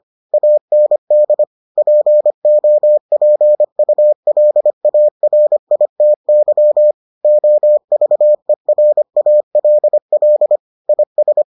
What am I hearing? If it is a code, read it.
Morse code, heard as a keyed tone, about 25 wpm, 600 Hz. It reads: AND POPULARITY OVERALL IS